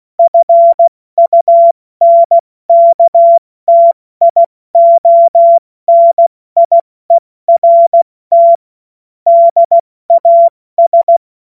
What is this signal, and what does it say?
Morse code, heard as a keyed tone, about 16 wpm, 680 Hz. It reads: FUNKTIONIERT DAS
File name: Morsekurs / Witze / Witz 25. Oktober 2022 70 F80